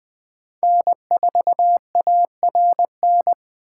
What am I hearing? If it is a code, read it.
Morse code, heard as a keyed tone, about 20 wpm, 700 Hz. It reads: N4ARN
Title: Speed Racing - USA 1x3 Call Signs 17wpm